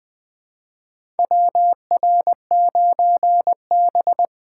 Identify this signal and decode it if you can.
Morse code, heard as a keyed tone, about 21 wpm, 700 Hz. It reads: WR9B
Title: USA Call Signs - 4 Characters - 2x Character Spacing - 20wpm